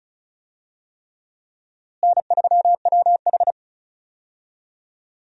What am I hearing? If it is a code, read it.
Morse code, heard as a keyed tone, about 36 wpm, 700 Hz. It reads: N3WH
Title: USA Call Signs - 4 Characters - Repeated 3x - 35wpm